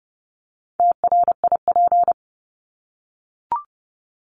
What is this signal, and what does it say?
Morse code, heard as a keyed tone, about 30 wpm, 700 Hz. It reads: TRIP
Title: Top 1251-1500 Words 30wpm